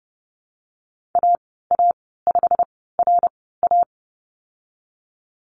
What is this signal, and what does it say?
Morse code, heard as a keyed tone, about 30 wpm, spaced as for 17 wpm, 700 Hz. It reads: AA5RA
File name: Lesson 19g - Letter w - Call Signs 30wpm